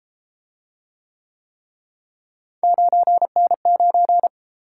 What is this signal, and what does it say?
Morse code, heard as a keyed tone, about 31 wpm, 700 Hz. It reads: 9N9